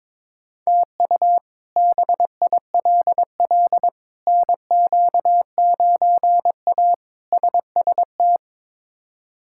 Morse code, heard as a keyed tone, about 22 wpm, 700 Hz. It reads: TU BILL NQ9A SST
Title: K1USN SST QSO Practice - 22wpm